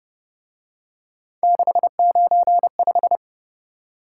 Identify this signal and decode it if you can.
Morse code, heard as a keyed tone, about 29 wpm, 700 Hz. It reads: 695